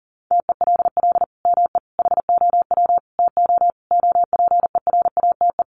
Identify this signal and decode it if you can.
Morse code, heard as a keyed tone, about 40 wpm, 700 Hz. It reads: TELL ME HOW TO OPERATE
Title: Sentences from Top 1500 Words 40wpm